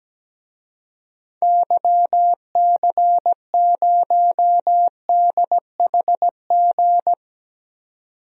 Morse code, heard as a keyed tone, about 17 wpm, 700 Hz. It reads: YC0DHG
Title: DXCC-100 Call Signs - 6 Characters - 17wpm